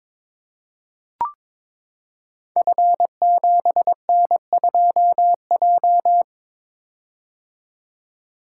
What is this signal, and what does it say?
Morse code, heard as a keyed tone, about 23 wpm, 700 Hz. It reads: F7N2J
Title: Code Groups - 5 Letter:Numbers - 22wpm